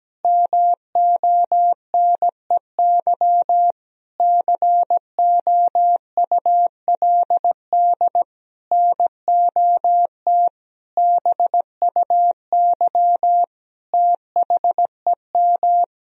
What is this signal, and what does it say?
Morse code, heard as a keyed tone, about 17 wpm, 700 Hz. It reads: MONEY COULD NOT BUY THEM
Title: Sentences from Top 1000 Words - Review Part 4 - 17wpm